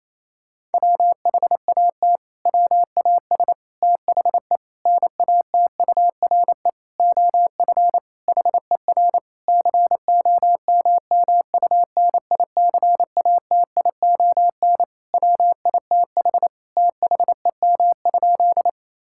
Morse code, heard as a keyed tone, about 28 wpm, 700 Hz. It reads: WHAT WAS THE NATURE OF HER COMMUNICATION WITH THEM?
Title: Sentences from Top 1000 Words - Review Part 6 - 28wpm